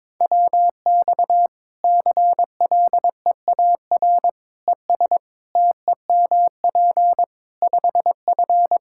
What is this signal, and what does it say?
Morse code, heard as a keyed tone, about 21 wpm, 700 Hz. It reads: WX CLEAR ES TEMP 5F
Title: Practice QSO - Exchange 2 - Info - 22wpm